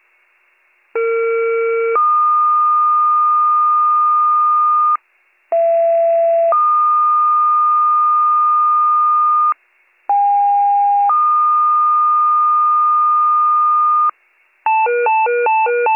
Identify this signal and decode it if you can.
Two-tone page, heard as a one-second tone, then a three-second tone, about 460 then 1220 Hz, about 670 then 1220 Hz, about 810 then 1220 Hz. It is heard from 0.9 to 14.1 s.